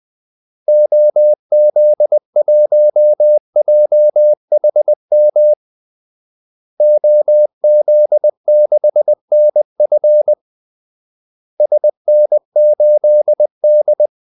Morse code, heard as a keyed tone, about 20 wpm, 600 Hz. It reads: OZ1JHM OZ6NF SN8D
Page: Kaldesignaler 20 wpm